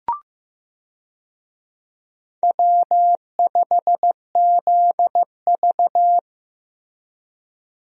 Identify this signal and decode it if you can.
Morse code, heard as a keyed tone, about 15 wpm, 700 Hz. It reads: W5ZV
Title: USA Call Signs - 4 Characters - Repeated 2x - 15wpm